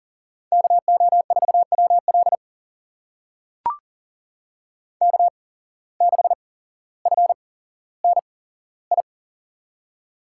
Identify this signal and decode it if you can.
Morse code, heard as a keyed tone, about 38 wpm, spaced as for 17 wpm, 700 Hz. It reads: KO4WL K6FNI
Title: USA Call Signs - 5 Characters - 8x Character Spacing - 40wpm